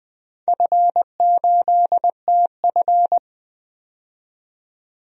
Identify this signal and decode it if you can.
Morse code, heard as a keyed tone, about 20 wpm, 700 Hz. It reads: F8TF
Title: DXCC-100 Call Signs - 4 Characters - 20wpm